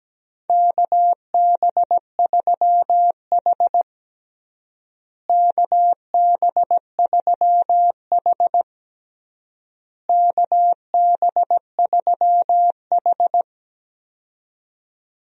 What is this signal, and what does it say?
Morse code, heard as a keyed tone, about 17 wpm, 700 Hz. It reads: KB3H KB3H KB3H
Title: USA Call Signs - 4 Characters - Repeated 3x - 17wpm